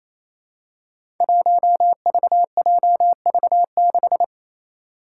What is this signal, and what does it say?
Morse code, heard as a keyed tone, about 27 wpm, 700 Hz. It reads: 1VJV6
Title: Five Challenging Letter-Number Combinations 28wpm